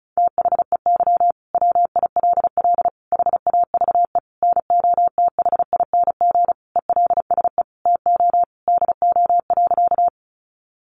Morse code, heard as a keyed tone, about 34 wpm, 700 Hz. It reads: THEY WILL HAVE NOTHING ELSE TO DO.
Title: Sentences from Top 800 Words 35wpm